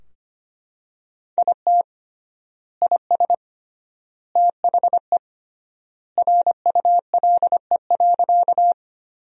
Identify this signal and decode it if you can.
Morse code, heard as a keyed tone, about 25 wpm, 700 Hz. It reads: IT IS THE RULE.
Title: 3x Word-Spacing - Sentences from Top 400 Words - 25wpm